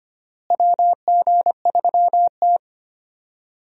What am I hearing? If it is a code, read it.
Morse code, heard as a keyed tone, about 25 wpm, 700 Hz. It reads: WG3T
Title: USA Call Signs - 4 Characters - Repeated 3x - 25wpm